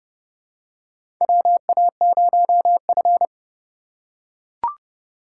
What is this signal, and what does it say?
Morse code, heard as a keyed tone, about 30 wpm, 700 Hz. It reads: WA0F
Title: USA Call Signs - 4 Characters - 2x Character Spacing - 30wpm